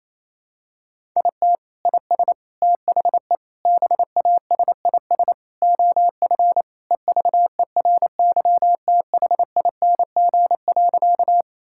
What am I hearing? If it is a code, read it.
Morse code, heard as a keyed tone, about 28 wpm, 700 Hz. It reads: IT IS THE BASIS OF EVERYTHING.
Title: Sentences from Top 1000 Words - Review Part 5 - 28wpm